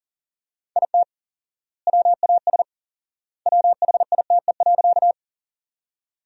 Morse code, heard as a keyed tone, about 37 wpm, 700 Hz. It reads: IT WAS WHITE.
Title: 4x Word-Spacing - Sentences from Top 300 Words - 40wpm